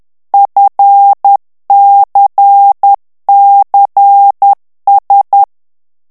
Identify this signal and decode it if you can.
Morse code, heard as a keyed tone, about 10 wpm, 800 Hz. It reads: FCCS
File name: ANGLAIS texte clair 3 vitesse 10 mots minute